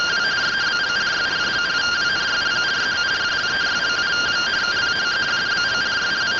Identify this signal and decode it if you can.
Two-tone teleprinter signal, about 170 Hz shift.